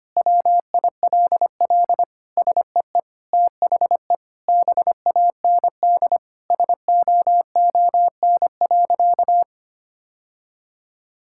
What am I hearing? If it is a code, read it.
Morse code, heard as a keyed tone, about 25 wpm, 700 Hz. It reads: WILL SEE THE BAND SOON.
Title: Sentences from Top 1600 Words 25wpm